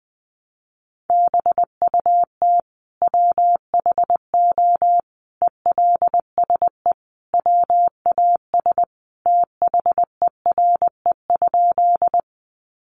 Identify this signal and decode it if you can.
Morse code, heard as a keyed tone, about 20 wpm, 700 Hz. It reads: BUT WHO ELSE WAS THERE?
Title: Sentences from Top 800 Words 20wpm